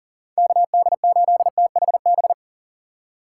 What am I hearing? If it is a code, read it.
Morse code, heard as a keyed tone, about 40 wpm, 700 Hz. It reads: KD8THB
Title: USA Call Signs - 6 Characters - 2x Character Spacing - 40wpm